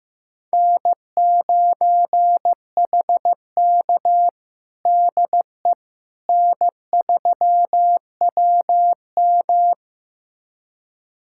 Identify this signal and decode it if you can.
Morse code, heard as a keyed tone, about 15 wpm, 700 Hz. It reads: N9HK DE N3WM
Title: USA Callsign DE Callsign - 4 Characters - 15wpm